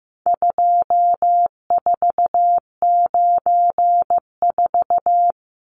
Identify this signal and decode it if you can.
Morse code, heard as a keyed tone, about 15 wpm, 700 Hz. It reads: 2494
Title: Numbers - 4-Digit 15wpm